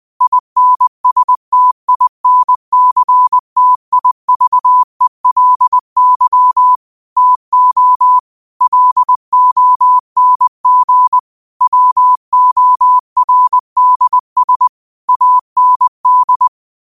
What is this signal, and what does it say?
Morse code, heard as a keyed tone, about 20 wpm, 1000 Hz. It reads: INSTINCTIVELY TO LONG WORDS AND
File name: qotdfiles / 20wpm